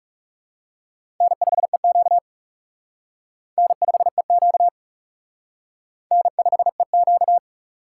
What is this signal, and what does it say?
Morse code, heard as a keyed tone, about 38 wpm, 700 Hz. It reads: N5EQ N5EQ N5EQ
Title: Speed Racing - USA 1x2 Call Signs 30wpm